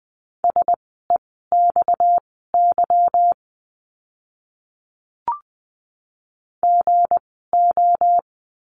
Morse code, heard as a keyed tone, about 20 wpm, spaced as for 14 wpm, 700 Hz. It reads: SEXY GO
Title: 230 Lesson 37f - Letter x - Words 20wpm